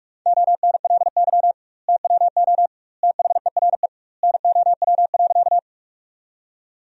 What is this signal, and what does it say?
Morse code, heard as a keyed tone, about 46 wpm, 700 Hz. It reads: ONLY TWO THERE NOW.
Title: 2x Word-Spacing - Sentences from Top 200 Words - 45wpm